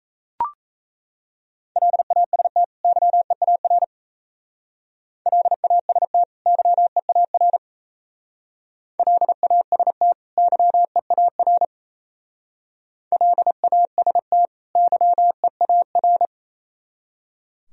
Morse code, heard as a keyed tone, about 36 wpm, 690 Hz. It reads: LAST YEAR LAST YEAR LAST YEAR LAST YEAR